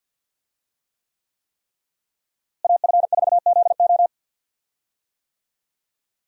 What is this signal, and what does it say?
Morse code, heard as a keyed tone, about 52 wpm, 700 Hz. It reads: AF4ZO
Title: USA Call Signs - 5 Characters - Repeated 3x - 50wpm